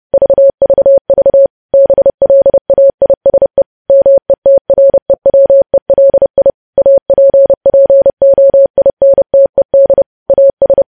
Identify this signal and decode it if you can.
Morse code, heard as a keyed tone, about 30 wpm, 550 Hz. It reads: VVV BLAISE METREWELI APPOINTED AS
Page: news headlines in morse code at 30 wpm